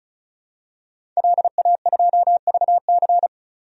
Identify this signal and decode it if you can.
Morse code, heard as a keyed tone, about 36 wpm, 700 Hz. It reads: LA2VC
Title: DXCC-100 Call Signs - 5 Characters - 35wpm